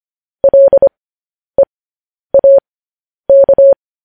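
Morse code, heard as a keyed tone, about 25 wpm, spaced as for 10 wpm, 550 Hz. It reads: LEAK
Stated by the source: Morse news headlines practice at 10.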